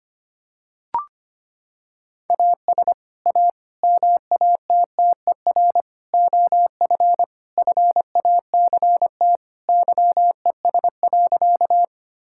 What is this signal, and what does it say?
Morse code, heard as a keyed tone, about 25 wpm, 700 Hz. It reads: AS A MATTER OF FACT YES.
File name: practice / 25wpm / Sentences with 4-Word Phrases - 25wpm